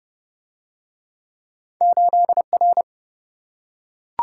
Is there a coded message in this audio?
Yes: Morse code from 1.8 to 2.8 s, heard as a keyed tone, about 27 wpm, 700 Hz. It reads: 8R